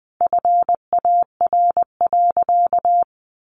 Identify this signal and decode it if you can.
Morse code, heard as a keyed tone, about 20 wpm, 700 Hz. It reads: FAR.
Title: Sentences from Top 400 Words 20wpm